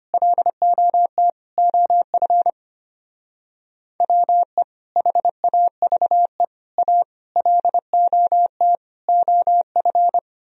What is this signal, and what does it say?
Morse code, heard as a keyed tone, about 29 wpm, 700 Hz. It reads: LOT OF WE HAVE A LOT OF
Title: Speed-Racing - Sets of 5 Words 25wpm